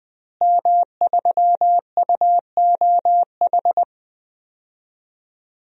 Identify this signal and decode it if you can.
Morse code, heard as a keyed tone, about 20 wpm, 700 Hz. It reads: M3UOH
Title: DXCC-100 Call Signs - 5 Characters - Repeated 2x - 20wpm